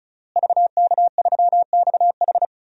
Morse code, heard as a keyed tone, about 36 wpm, 700 Hz. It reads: VK3XH
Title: DXCC-100 Call Signs - 5 Characters - Repeated 3x - 35wpm